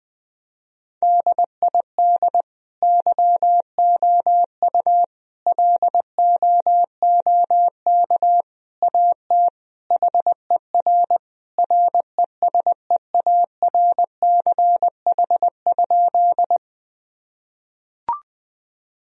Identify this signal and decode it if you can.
Morse code, heard as a keyed tone, about 20 wpm, 700 Hz. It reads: DID YOU LOOK AT HER RESEARCH?
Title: Sentences from Top 1000 Words - Review Part 4 - 20wpm